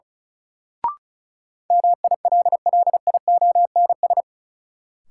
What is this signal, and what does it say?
Morse code, heard as a keyed tone, about 36 wpm, 700 Hz. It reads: MILLIONS
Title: Sets of 5 Words - New Word Drill - 35wpm